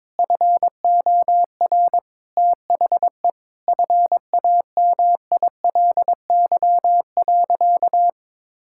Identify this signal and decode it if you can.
Morse code, heard as a keyed tone, about 22 wpm, 700 Hz. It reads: FOR THE FAMILY.
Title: Sentences from Top 1600 Words 22wpm